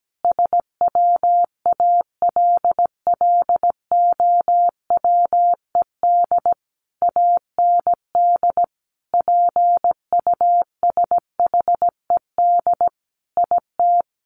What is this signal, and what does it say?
Morse code, heard as a keyed tone, about 17 wpm, 700 Hz. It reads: SWALLOWED AND PUSHED IT
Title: A Sound Of Thunder 17wpm